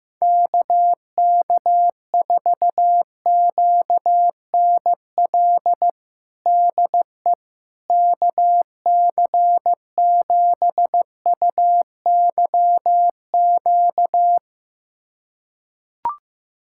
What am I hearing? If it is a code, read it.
Morse code, heard as a keyed tone, about 16 wpm, 700 Hz. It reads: KK4QNL DE KC7UYQ
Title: USA Callsign DE Callsign - 6 Characters - 15wpm